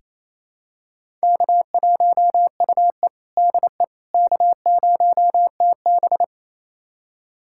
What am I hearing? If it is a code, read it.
Morse code, heard as a keyed tone, about 28 wpm, 700 Hz. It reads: K1UE DE K0TB
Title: USA Callsign DE Callsign - 4 Characters - 28wpm